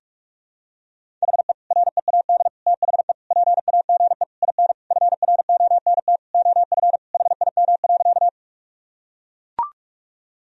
Morse code, heard as a keyed tone, about 41 wpm, 700 Hz. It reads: HE READ THE PAGE IN FRONT OF HIM.
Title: Sentences from Top 1000 Words - Review Part 1 - 45wpm